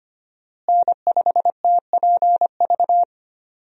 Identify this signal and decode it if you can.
Morse code, heard as a keyed tone, about 25 wpm, 700 Hz. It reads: N5TPV
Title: USA Call Signs - 5 Characters - Repeated 2x - 25wpm